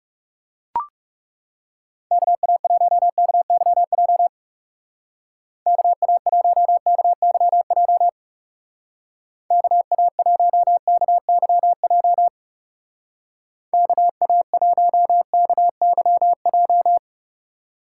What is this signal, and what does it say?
Morse code, heard as a keyed tone, about 36 wpm, 700 Hz. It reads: KA1KYJ KA1KYJ KA1KYJ KA1KYJ
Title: Speed Racing - USA Call Signs 30wpm